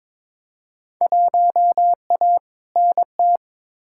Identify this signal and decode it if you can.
Morse code, heard as a keyed tone, about 22 wpm, 700 Hz. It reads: 1A NT
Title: Field Day Exchanges Encore - 22wpm